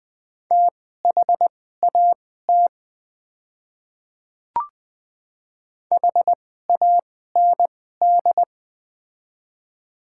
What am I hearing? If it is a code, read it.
Morse code, heard as a keyed tone, about 20 wpm, spaced as for 14 wpm, 700 Hz. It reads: THAT HAND
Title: 2x Character Spacing - Top 200 Words - 20wpm